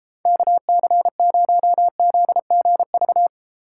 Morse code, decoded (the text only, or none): KC0ZGV